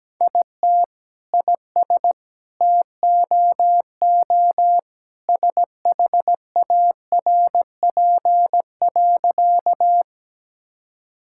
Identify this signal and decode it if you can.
Morse code, heard as a keyed tone, about 17 wpm, 700 Hz. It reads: IT IS TOO SHARP.
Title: Sentences from Top 1600 Words 17wpm